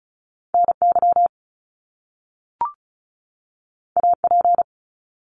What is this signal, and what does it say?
Morse code, heard as a keyed tone, about 36 wpm, 700 Hz. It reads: NY AP
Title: Top Two Letter Combinations in Words 35wpm